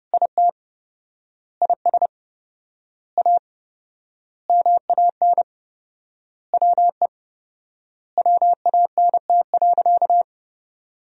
Morse code, heard as a keyed tone, about 30 wpm, 700 Hz. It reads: IT IS A MAN WE WANT.
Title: 4x Word-Spacing - Sentences from Top 200 Words - 30wpm